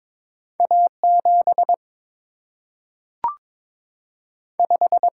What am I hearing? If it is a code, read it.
Morse code, heard as a keyed tone, about 22 wpm, 700 Hz. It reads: A 7 5
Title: Single-Letter Single-Number 22wpm